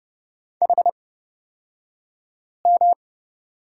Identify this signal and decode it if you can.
Morse code, heard as a keyed tone, about 30 wpm, spaced as for 5 wpm, 700 Hz. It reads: HM
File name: Lesson 37d - Letter x - Two Characters 30wpm